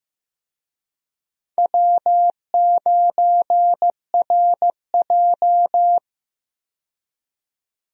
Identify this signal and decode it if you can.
Morse code, heard as a keyed tone, about 15 wpm, 700 Hz. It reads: W9RJ
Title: USA Call Signs - 4 Characters - 15wpm